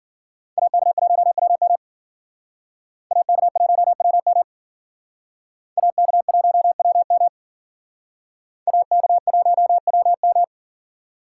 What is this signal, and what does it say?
Morse code, heard as a keyed tone, about 49 wpm, 700 Hz. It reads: AK1WM AK1WM AK1WM AK1WM
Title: Speed Racing - USA 2x2 Call Signs 40wpm